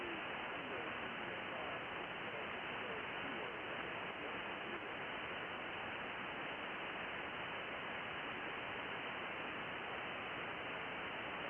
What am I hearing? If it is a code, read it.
FT8.
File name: contests / sept-vhf-k5tr- -2000